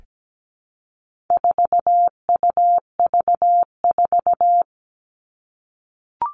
Morse code, heard as a keyed tone, about 17 wpm, 700 Hz. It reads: 4UV4
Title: All Cut Numbers - 4-Digit 17wpm